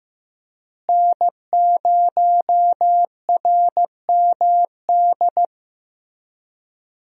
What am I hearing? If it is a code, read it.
Morse code, heard as a keyed tone, about 15 wpm, 700 Hz. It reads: N0RMD
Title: USA Call Signs - 5 Characters - Repeated 3x - 15wpm